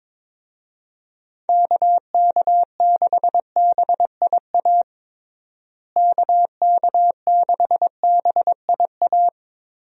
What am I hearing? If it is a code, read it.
Morse code, heard as a keyed tone, about 22 wpm, 700 Hz. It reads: KK6BIA KK6BIA
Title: USA Call Signs - 6 Characters - Repeated 3x - 22wpm